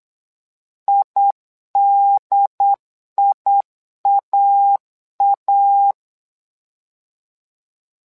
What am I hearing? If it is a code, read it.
Morse code, heard as a keyed tone, about 8 wpm, 800 Hz. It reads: IDIAA
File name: LEZIONI / 2 UHDN / 2 UHDN 8wpm2232